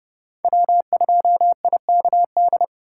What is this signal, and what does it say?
Morse code, heard as a keyed tone, about 32 wpm, 700 Hz. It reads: W2IKD
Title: USA Call Signs - 5 Characters - 2x Character Spacing - 30wpm